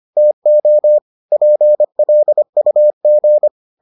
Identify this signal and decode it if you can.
Morse code, heard as a keyed tone, about 25 wpm, 600 Hz. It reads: TO PLUG